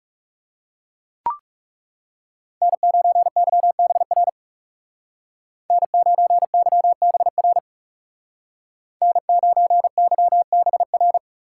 Morse code, heard as a keyed tone, about 35 wpm, 700 Hz. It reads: N9YBR N9YBR N9YBR
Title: Speed Racing - USA 1x3 Call Signs 30wpm